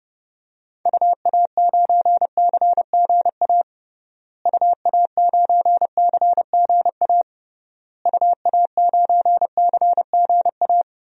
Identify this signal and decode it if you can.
Morse code, heard as a keyed tone, about 32 wpm, 700 Hz. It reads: UA9CGA UA9CGA UA9CGA
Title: DXCC-100 Call Signs - 6 Characters - Repeated 3x - 30wpm